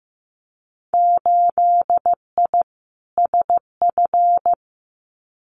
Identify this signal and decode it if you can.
Morse code, heard as a keyed tone, about 15 wpm, 700 Hz. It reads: 8I SF
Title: Winter Field Day Exchanges Encore 15wpm